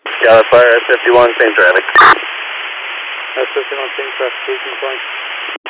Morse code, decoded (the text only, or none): none